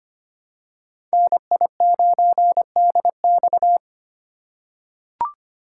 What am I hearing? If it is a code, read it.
Morse code, heard as a keyed tone, about 25 wpm, 700 Hz. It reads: NI9DX
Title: USA Call Signs - 5 Characters - 8x Character Spacing - 25wpm